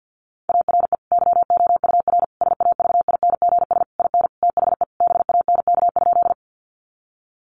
Morse code, heard as a keyed tone, about 50 wpm, 700 Hz. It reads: ARE YOUR SAVINGS IN THE BANK?
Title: Sentences from Top 1200 Words 50wpm